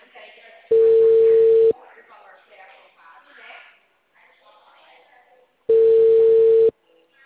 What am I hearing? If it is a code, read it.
Ringback tone (China, 450 Hz), with speech.